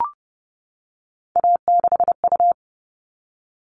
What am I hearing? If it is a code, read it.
Morse code, heard as a keyed tone, about 30 wpm, 700 Hz. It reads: A6U
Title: All Cut Numbers - 3-Digit 30wpm